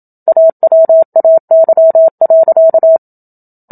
Morse code, heard as a keyed tone, about 27 wpm, 650 Hz. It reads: AWAY.
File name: Book Two / 01-Chapter 1 - 27 WPM